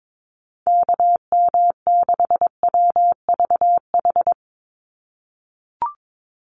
Morse code, heard as a keyed tone, about 22 wpm, 700 Hz. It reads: KM6WVH
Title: USA Call Signs 22wpm